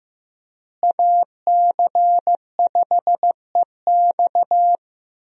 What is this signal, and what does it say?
Morse code, heard as a keyed tone, about 16 wpm, 700 Hz. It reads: AC5EX
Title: USA Call Signs - 5 Characters - Repeated 2x - 15wpm